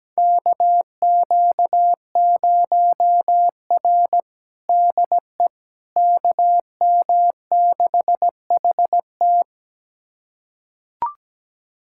Morse code, heard as a keyed tone, about 17 wpm, 700 Hz. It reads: KQ0R DE KM6HT
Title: USA Callsign DE Callsign 17wpm